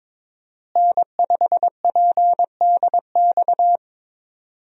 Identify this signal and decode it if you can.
Morse code, heard as a keyed tone, about 22 wpm, 700 Hz. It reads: N5PDX
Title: USA Call Signs - 5 Characters - Repeated 3x - 22wpm